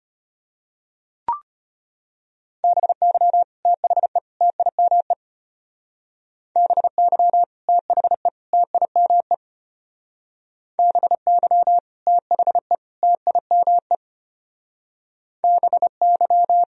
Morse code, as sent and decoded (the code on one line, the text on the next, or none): -... -.-- / - .... / - .. -- . / -... -.-- / - .... / - .. -- . / -... -.-- / - .... / - .. -- . / -... -.--
BY THE TIME BY THE TIME BY THE TIME BY